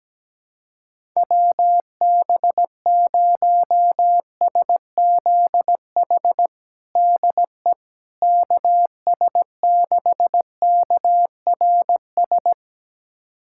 Morse code, heard as a keyed tone, about 17 wpm, 700 Hz. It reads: WB0SZH DE KS6KRS
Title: USA Callsign DE Callsign 17wpm